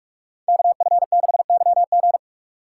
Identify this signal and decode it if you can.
Morse code, heard as a keyed tone, about 54 wpm, 700 Hz. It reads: KF6YG
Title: Speed Racing - USA 2x2 Call Signs 45wpm